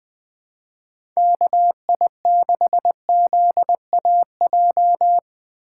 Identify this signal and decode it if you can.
Morse code, heard as a keyed tone, about 20 wpm, 700 Hz. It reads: KI6ZAJ
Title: USA Call Signs - 6 Characters - 2x Character Spacing - 20wpm